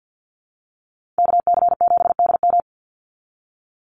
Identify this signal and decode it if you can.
Morse code, heard as a keyed tone, about 50 wpm, 700 Hz. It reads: KC7DM
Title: NA SOTA Activator Callsigns 50wpm